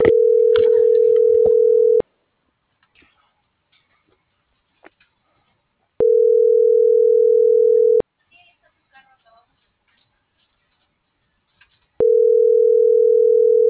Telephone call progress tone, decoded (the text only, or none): ringback tone